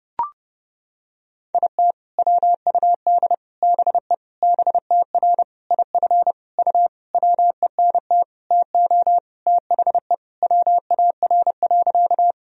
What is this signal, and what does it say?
Morse code, heard as a keyed tone, about 32 wpm, 700 Hz. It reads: IT WUD BE BTR IF U WENT TO THE WAR.
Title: Sentences from Top 400 Words and Abbreviations 30wpm